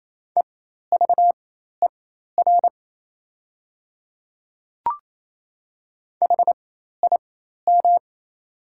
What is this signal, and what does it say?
Morse code, heard as a keyed tone, about 28 wpm, spaced as for 13 wpm, 700 Hz. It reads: EVER HIM
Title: 4x Character Spacing - Top 500 Words - 28wpm